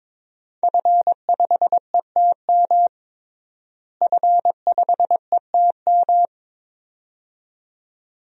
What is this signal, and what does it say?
Morse code, heard as a keyed tone, about 22 wpm, 700 Hz. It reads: F5ETM F5ETM
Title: DXCC-100 Call Signs - 5 Characters - Repeated 2x - 22wpm